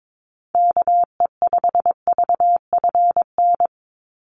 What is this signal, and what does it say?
Morse code, heard as a keyed tone, about 22 wpm, 700 Hz. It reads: KE5VFN
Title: USA Call Signs 22wpm